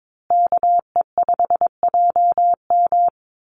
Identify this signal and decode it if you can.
Morse code, heard as a keyed tone, about 22 wpm, 700 Hz. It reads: KE5JM
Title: USA Call Signs Repeated 2x 22wpm